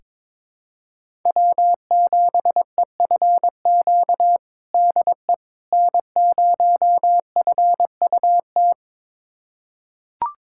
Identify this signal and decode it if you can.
Morse code, heard as a keyed tone, about 22 wpm, 700 Hz. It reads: W7EFQ DE N0FUT
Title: USA Callsign DE Callsign - 5 Characters - 22wpm